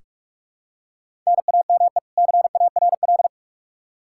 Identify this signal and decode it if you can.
Morse code, heard as a keyed tone, about 42 wpm, 700 Hz. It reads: NAME CARL
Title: Practice QSO - Exchange 1 - Beginning - 45wpm